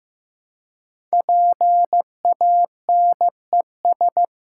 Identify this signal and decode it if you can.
Morse code, heard as a keyed tone, about 15 wpm, 700 Hz. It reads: PANES
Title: Code Groups - 5 Letters - 15wpm